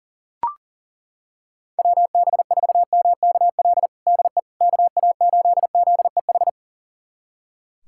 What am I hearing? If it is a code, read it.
Morse code, heard as a keyed tone, about 40 wpm, 700 Hz. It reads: WB4MKL DE KA8ZEH
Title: USA Callsign DE Callsign - 6 Characters - 40wpm